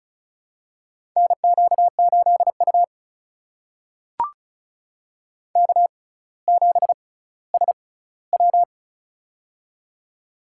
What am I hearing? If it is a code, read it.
Morse code, heard as a keyed tone, about 36 wpm, spaced as for 18 wpm, 700 Hz. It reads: NQ8U K7SW
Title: USA Call Signs - 4 Characters - 6x Character Spacing - 35wpm